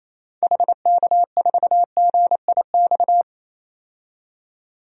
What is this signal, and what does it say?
Morse code, heard as a keyed tone, about 28 wpm, 700 Hz. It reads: HK4GIX
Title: DXCC-100 Call Signs - 6 Characters - 28wpm